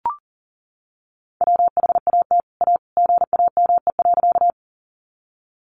Morse code, heard as a keyed tone, about 40 wpm, 700 Hz. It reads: WHAT A GAME.
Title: Sentences from Top 500 Words 40wpm